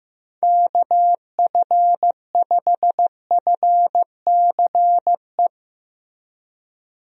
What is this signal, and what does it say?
Morse code, heard as a keyed tone, about 15 wpm, 700 Hz. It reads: KF5FCE
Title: USA Call Signs - 6 Characters - Repeated 2x - 15wpm